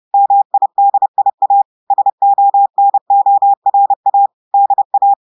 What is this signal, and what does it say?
Morse code, heard as a keyed tone, about 30 wpm, 800 Hz. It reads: MIDIA SONORA DA
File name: audios / portugais / portugais texte en clair 1 vitesse 30 mots minutes